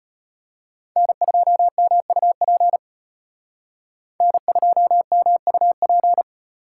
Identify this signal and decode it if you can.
Morse code, heard as a keyed tone, about 32 wpm, 700 Hz. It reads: N2MUP N2MUP